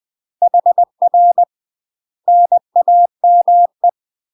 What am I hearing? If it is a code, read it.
Morse code, heard as a keyed tone, about 20 wpm, 700 Hz. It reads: HR NAME
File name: UK MASTER QSOs TRAINING / 20WPM / UK MASTER QSOs 0050